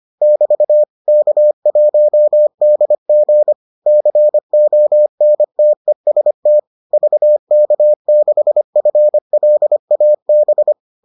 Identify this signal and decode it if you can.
Morse code, heard as a keyed tone, about 25 wpm, 600 Hz. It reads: <BT> K1DG CONTEST VK6FLAB